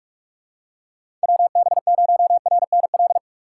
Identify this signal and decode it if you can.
Morse code, heard as a keyed tone, about 44 wpm, 700 Hz. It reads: WB0RNL
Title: USA Call Signs - 6 Characters - 6x Character Spacing - 45wpm